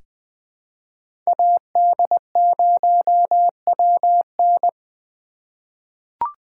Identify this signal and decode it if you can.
Morse code, heard as a keyed tone, about 20 wpm, 700 Hz. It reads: AD0WN
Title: 2023 POTA Activator Call Signs - 20wpm